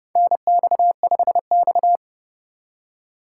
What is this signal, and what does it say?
Morse code, heard as a keyed tone, about 30 wpm, 700 Hz. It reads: NX5X
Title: USA Call Signs - 4 Characters - 6x Character Spacing - 30wpm